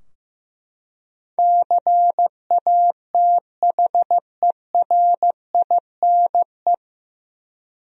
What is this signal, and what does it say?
Morse code, heard as a keyed tone, about 15 wpm, 700 Hz. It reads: CATHERINE
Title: USA Common Names - 15wpm